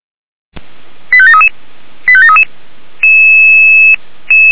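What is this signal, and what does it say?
Five-tone selective call (ZVEI), heard as quick runs of five tones, tones of about 70 ms. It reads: 85733 85733
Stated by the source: ZVEI five-tone call received on an SDR.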